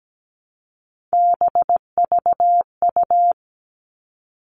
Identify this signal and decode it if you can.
Morse code, heard as a keyed tone, about 17 wpm, 700 Hz. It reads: BVU
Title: All Cut Numbers - 3-Digit 17wpm